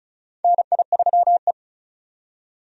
Morse code, heard as a keyed tone, about 36 wpm, 700 Hz. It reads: NI3E
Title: USA Call Signs - 4 Characters - 35wpm